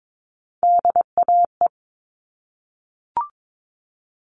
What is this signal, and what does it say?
Morse code, heard as a keyed tone, about 22 wpm, 700 Hz. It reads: DAE
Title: All Cut Numbers - 3-Digit 22wpm